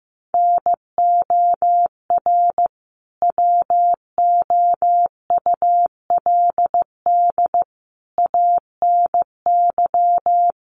Morse code, heard as a keyed tone, about 15 wpm, 700 Hz. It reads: NOR WOULD ANY
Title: Sentences from Top 1500 Words 15wpm